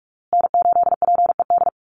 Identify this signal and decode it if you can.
Morse code, heard as a keyed tone, about 44 wpm, 700 Hz. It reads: N8PED
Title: USA Call Signs Repeated 2x 45wpm